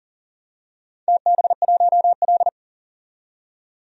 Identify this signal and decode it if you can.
Morse code, heard as a keyed tone, about 40 wpm, 700 Hz. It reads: TB1L
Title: DXCC-100 Call Signs - 4 Characters - 40wpm